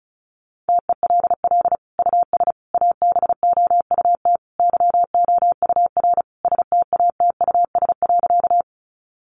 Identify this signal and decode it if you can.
Morse code, heard as a keyed tone, about 34 wpm, 700 Hz. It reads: TELL US ABOUT YOUR STATUS.
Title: Sentences from Top 1400 Words 35wpm